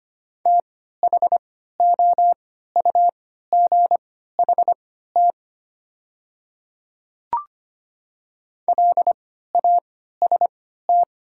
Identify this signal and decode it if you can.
Morse code, heard as a keyed tone, about 25 wpm, spaced as for 14 wpm, 690 Hz. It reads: THOUGHT LAST
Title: 3x Character Spacing - Top 300 Words - 25wpm